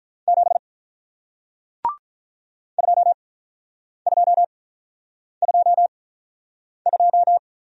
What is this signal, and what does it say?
Morse code, heard as a keyed tone, about 48 wpm, spaced as for 9 wpm, 700 Hz. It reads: 7 2222